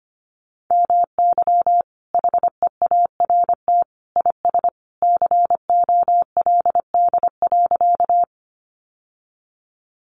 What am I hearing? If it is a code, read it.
Morse code, heard as a keyed tone, about 25 wpm, 700 Hz. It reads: MY HEART IS COLD.
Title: Sentences from Top 1200 Words 25wpm